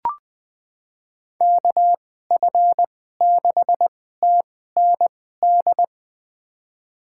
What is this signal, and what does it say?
Morse code, heard as a keyed tone, about 20 wpm, spaced as for 14 wpm, 700 Hz. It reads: KF6TND
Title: USA Call Signs - 6 Characters - 2x Character Spacing - 20wpm